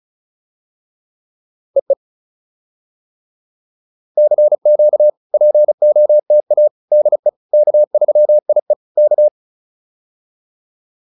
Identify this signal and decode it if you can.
Morse code, heard as a keyed tone, about 34 wpm, 600 Hz. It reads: EE CQ POTA DE K3IE K
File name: POTA307 35